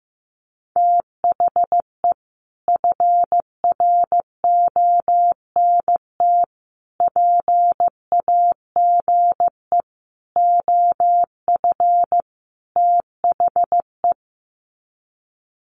Morse code, heard as a keyed tone, about 15 wpm, 700 Hz. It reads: THE FRONT PAGE OF THE
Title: Sets of 5 Words Encore 15wpm